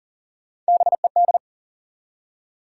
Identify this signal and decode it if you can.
Morse code, heard as a keyed tone, about 42 wpm, 700 Hz. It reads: BED
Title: Three Letter Words - 40wpm